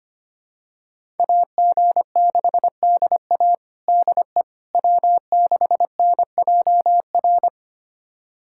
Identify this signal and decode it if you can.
Morse code, heard as a keyed tone, about 25 wpm, 700 Hz. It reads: AG6DA DE W6NJR
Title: USA Callsign DE Callsign - 5 Characters - 25wpm